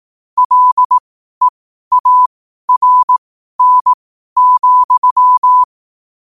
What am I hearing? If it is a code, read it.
Morse code, heard as a keyed tone, about 18 wpm, spaced as for 12 wpm, 1000 Hz. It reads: LEARN,
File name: qotdfiles / 12wpm